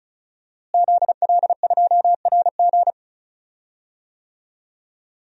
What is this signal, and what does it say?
Morse code, heard as a keyed tone, about 36 wpm, 700 Hz. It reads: ZL2RG
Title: DXCC-100 Call Signs - 5 Characters - Repeated 3x - 35wpm